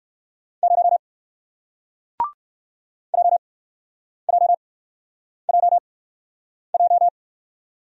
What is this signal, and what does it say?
Morse code, heard as a keyed tone, about 58 wpm, spaced as for 9 wpm, 700 Hz. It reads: 0 JJJJ